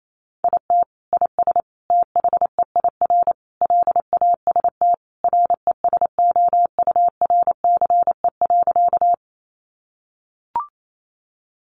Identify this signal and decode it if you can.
Morse code, heard as a keyed tone, about 28 wpm, 700 Hz. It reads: IT IS THEIR LAST RESOURCE.
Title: Sentences from Top 1300 Words 28wpm